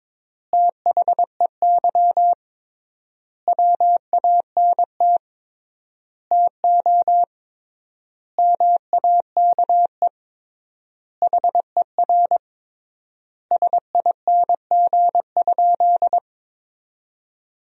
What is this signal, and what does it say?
Morse code, heard as a keyed tone, about 21 wpm, 700 Hz. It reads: THEY WANT TO MAKE HER SING?
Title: 3x Word-Spacing - Sentences from Top 400 Words - 22wpm